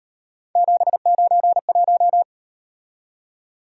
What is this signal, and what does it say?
Morse code, heard as a keyed tone, about 38 wpm, 700 Hz. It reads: 791